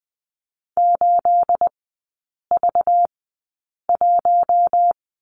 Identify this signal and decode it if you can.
Morse code, heard as a keyed tone, about 20 wpm, spaced as for 8 wpm, 690 Hz. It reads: 8V1
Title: 212 Lesson 34e - Number 8 - Three Characters 20wpm